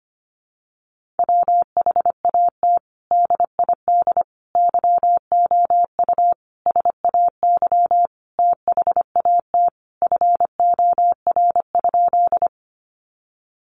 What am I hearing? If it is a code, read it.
Morse code, heard as a keyed tone, about 25 wpm, 700 Hz. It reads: WHAT DID YOU SAY THAT FOR?
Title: Sentences from Top 200 Words 25wpm